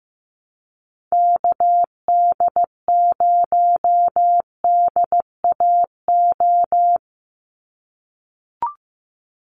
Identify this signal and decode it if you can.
Morse code, heard as a keyed tone, about 15 wpm, 700 Hz. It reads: KD0DAO
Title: USA Call Signs 15wpm